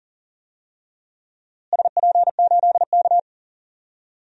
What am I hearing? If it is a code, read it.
Morse code, heard as a keyed tone, about 38 wpm, 700 Hz. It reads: SP8K
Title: DXCC-100 Call Signs - 4 Characters - Repeated 3x - 40wpm